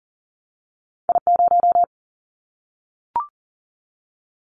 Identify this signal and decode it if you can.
Morse code, heard as a keyed tone, about 41 wpm, 700 Hz. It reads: I0
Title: Callsign Prefixes for Top 50 DXCC - 40wpm